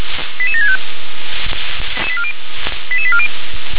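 Five-tone selective call (ZVEI), heard as quick runs of five tones, tones of about 70 ms. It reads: 88765 88840
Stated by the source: SDR-received ZVEI five-tone selective call, code 88707.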